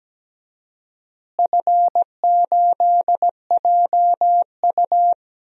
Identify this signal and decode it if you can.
Morse code, heard as a keyed tone, about 17 wpm, 700 Hz. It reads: F8JU
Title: DXCC-100 Call Signs - 4 Characters - Repeated 2x - 17wpm